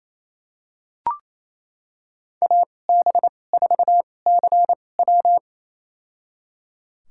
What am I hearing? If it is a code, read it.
Morse code, heard as a keyed tone, about 28 wpm, spaced as for 20 wpm, 700 Hz. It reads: AB4CW
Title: USA Call Signs - 5 Characters - 2x Character Spacing - 28wpm